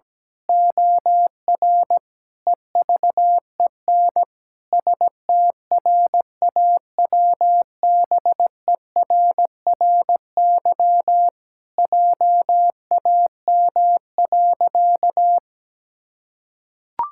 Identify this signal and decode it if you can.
Morse code, heard as a keyed tone, about 17 wpm, 700 Hz. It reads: OR EVEN STRAWBERRY JAM.
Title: Pangram Challenge v2 - 17wpm